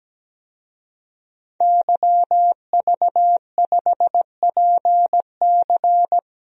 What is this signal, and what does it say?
Morse code, heard as a keyed tone, about 17 wpm, 700 Hz. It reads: YV5PC
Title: DXCC-100 Call Signs - 5 Characters - Repeated 3x - 17wpm